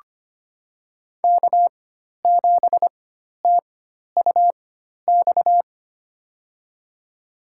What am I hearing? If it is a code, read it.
Morse code, heard as a keyed tone, about 25 wpm, spaced as for 12 wpm, 700 Hz. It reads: K7TUX
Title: USA Call Signs - 5 Characters - 4x Character Spacing - 25wpm